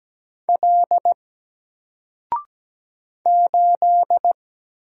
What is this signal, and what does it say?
Morse code, heard as a keyed tone, about 17 wpm, 700 Hz. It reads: L 8
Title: Single Letter-Number Alt 17wpm